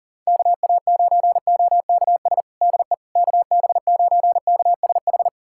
Morse code, heard as a keyed tone, about 40 wpm, 700 Hz. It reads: KA9OKS DE KB9KSH
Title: USA Callsign DE Callsign - 6 Characters - 40wpm